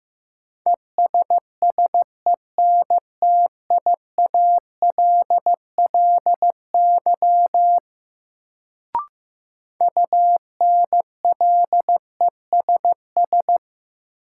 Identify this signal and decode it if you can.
Morse code, heard as a keyed tone, about 15 wpm, 700 Hz. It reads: ESSENTIALLY UNLESS
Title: Sentences from Top 1300 Words - New Word Drill Alt - 15wpm